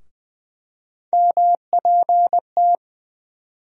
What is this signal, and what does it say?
Morse code, heard as a keyed tone, about 20 wpm, 700 Hz. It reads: MPT
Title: USA Call Sign Postfixes - 3 Letters 20wpm